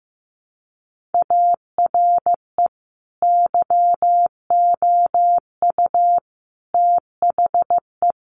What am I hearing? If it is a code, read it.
Morse code, heard as a keyed tone, about 15 wpm, 700 Hz. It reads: ARE YOU THE
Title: Sentences from Top 1500 Words 15wpm